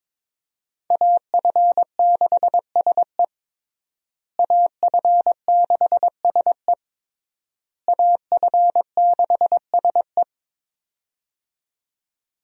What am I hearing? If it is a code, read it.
Morse code, heard as a keyed tone, about 22 wpm, 700 Hz. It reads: AF6SE AF6SE AF6SE
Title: USA Call Signs - 5 Characters - Repeated 3x - 22wpm